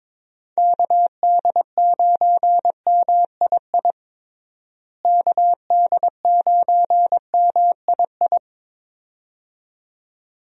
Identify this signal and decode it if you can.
Morse code, heard as a keyed tone, about 22 wpm, 700 Hz. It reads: KD9MII KD9MII
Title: USA Call Signs - 6 Characters - Repeated 2x - 22wpm